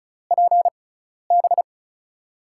Morse code, heard as a keyed tone, about 36 wpm, spaced as for 12 wpm, 700 Hz. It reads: PB
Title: USA Call Signs - 6 Characters - 6x Character Spacing - 35wpm